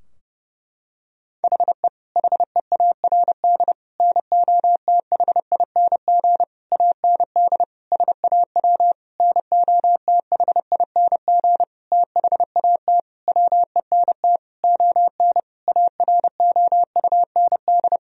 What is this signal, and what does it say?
Morse code, heard as a keyed tone, about 30 wpm, 700 Hz. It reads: HE HEARD NOTHING AND SAW NOTHING THAT WENT ON AROUND
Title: Sentences from Top 1000 Words - Review Part 3 - 30wpm